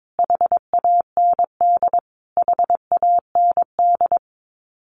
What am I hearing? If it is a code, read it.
Morse code, heard as a keyed tone, about 22 wpm, 700 Hz. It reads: HAND HAND
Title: Top 300 Words Repeated 22wpm